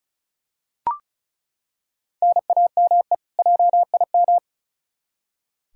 Morse code, heard as a keyed tone, about 36 wpm, 700 Hz. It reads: NAME JIM
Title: K1USN SST QSO Practice 35wpm